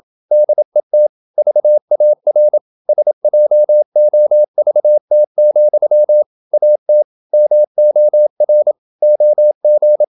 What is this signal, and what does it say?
Morse code, heard as a keyed tone, about 26 wpm, 600 Hz. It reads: DET VAR SJOVT, AT MOR OG
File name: Ferie QRQ 27wpm